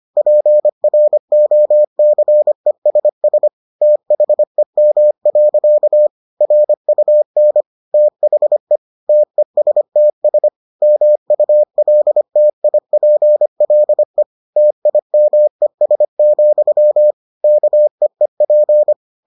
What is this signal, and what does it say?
Morse code, heard as a keyed tone, about 25 wpm, 600 Hz. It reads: PROCESS THEM. RUN THE TESTS MULTIPLE TIMES, KEEP